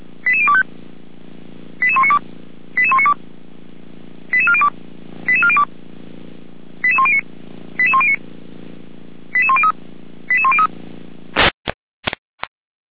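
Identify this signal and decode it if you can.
Five-tone selective call (ZVEI), heard as quick runs of five tones, tones of about 70 ms. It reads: 88025 88192 88192 88402 88402 88109 88109 88103 88103
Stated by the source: ZVEI five-tone selective call 88074 received by SDR.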